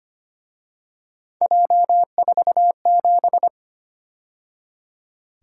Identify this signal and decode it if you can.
Morse code, heard as a keyed tone, about 29 wpm, 700 Hz. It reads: J47
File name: Speed-Racing - Three Letter-Number Combinations - 25wpm